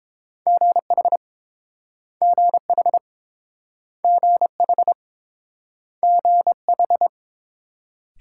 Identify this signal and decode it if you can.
Morse code, heard as a keyed tone, about 30 wpm, 700 Hz. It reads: GH GH GH GH